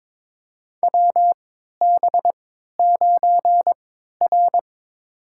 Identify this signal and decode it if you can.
Morse code, heard as a keyed tone, about 22 wpm, spaced as for 13 wpm, 700 Hz. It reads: WB9R
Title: USA Call Signs - 3x Character Spacing - 22wpm